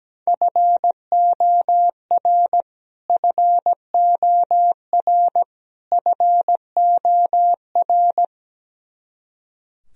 Morse code, heard as a keyed tone, about 17 wpm, 700 Hz. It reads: FOR FOR FOR
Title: Top 100 Words Repeated 17wpm